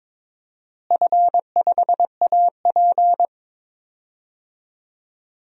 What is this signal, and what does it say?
Morse code, heard as a keyed tone, about 22 wpm, 700 Hz. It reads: F5AP
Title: DXCC-100 Call Signs - 4 Characters - Repeated 3x - 22wpm